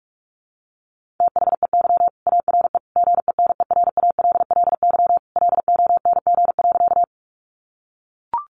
Morse code, heard as a keyed tone, about 44 wpm, 700 Hz. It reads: THEY ARE GENERALLY LONG.
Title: Sentences from Top 600 Words 45wpm